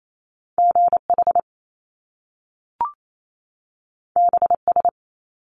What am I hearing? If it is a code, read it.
Morse code, heard as a keyed tone, about 28 wpm, 700 Hz. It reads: GH BS
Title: Top Two Letter Combinations in Words 28wpm